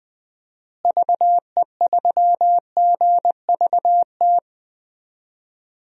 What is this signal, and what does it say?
Morse code, heard as a keyed tone, about 21 wpm, 700 Hz. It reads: VE3GVT
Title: DXCC-100 Call Signs - 6 Characters - Repeated 3x - 20wpm